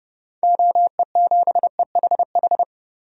Morse code, heard as a keyed tone, about 32 wpm, 700 Hz. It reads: OE7EHH
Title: DXCC-100 Call Signs - 6 Characters - Repeated 3x - 30wpm